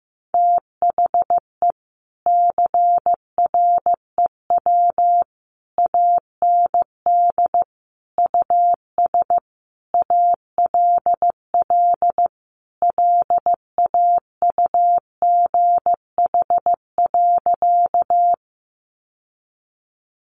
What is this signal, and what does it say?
Morse code, heard as a keyed tone, about 15 wpm, 700 Hz. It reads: THE CREW AND US ALL LAUGH.
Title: Sentences from Top 1500 Words 15wpm